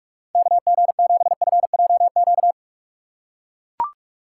Morse code, heard as a keyed tone, about 44 wpm, 700 Hz. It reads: KG7FJQ